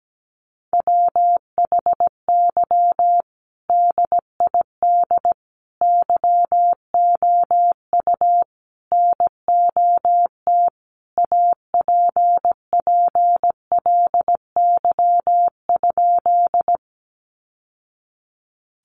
Morse code, heard as a keyed tone, about 17 wpm, 700 Hz. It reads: WHY DID YOU NOT APPLY?
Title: Sentences from Top 1200 Words 17wpm